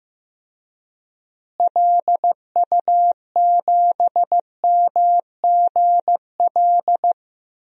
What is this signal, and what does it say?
Morse code, heard as a keyed tone, about 15 wpm, 700 Hz. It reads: LU7MGL
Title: DXCC-100 Call Signs - 6 Characters - Repeated 2x - 15wpm